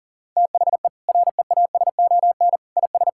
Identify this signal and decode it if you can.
Morse code, heard as a keyed tone, about 40 wpm, 700 Hz. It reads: THE REASON IS
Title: Sentences from Top 1600 Words 40wpm